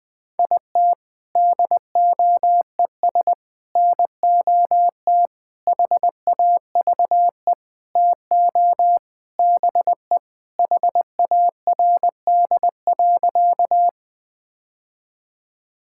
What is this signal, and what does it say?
Morse code, heard as a keyed tone, about 20 wpm, 700 Hz. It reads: IT DOES NOT HAVE TO BE HARD.
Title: Sentences with 4-Word Phrases - 20wpm